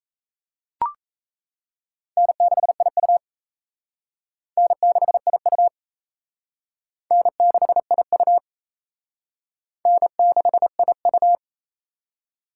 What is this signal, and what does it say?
Morse code, heard as a keyed tone, about 36 wpm, 700 Hz. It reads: N6IU N6IU N6IU N6IU